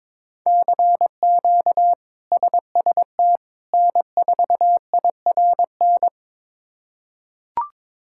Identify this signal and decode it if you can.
Morse code, heard as a keyed tone, about 22 wpm, 700 Hz. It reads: CQ SST N4IRN
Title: K1USN SST QSO Practice - 22wpm